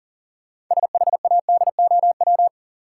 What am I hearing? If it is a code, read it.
Morse code, heard as a keyed tone, about 40 wpm, 700 Hz. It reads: SHADOW
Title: Sets of 5 Words Encore - New Word Drill - 40wpm